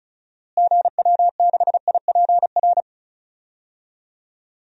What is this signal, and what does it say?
Morse code, heard as a keyed tone, about 36 wpm, 700 Hz. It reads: GW6IPR
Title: DXCC-100 Call Signs - 6 Characters - Repeated 3x - 35wpm